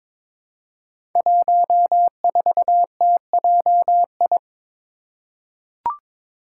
Morse code, heard as a keyed tone, about 22 wpm, 700 Hz. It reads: 14TJI
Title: Five Challenging Letter-Number Combinations 22wpm